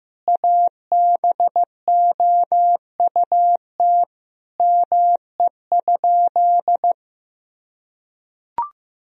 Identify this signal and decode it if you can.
Morse code, heard as a keyed tone, about 15 wpm, 700 Hz. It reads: ABOUT ME?
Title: Sentences from Top 1000 Words - Review Part 1 - 15wpm